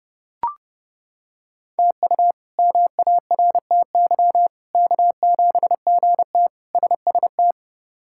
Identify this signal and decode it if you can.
Morse code, heard as a keyed tone, about 30 wpm, 700 Hz. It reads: TU MARTY K7GT SST
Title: K1USN SST QSO Practice - 30wpm